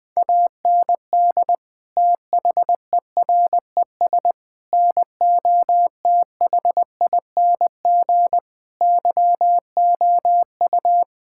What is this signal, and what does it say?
Morse code, heard as a keyed tone, about 20 wpm, 700 Hz. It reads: AND THERES NOTHING YOU
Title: Sets of 8 Words - 20wpm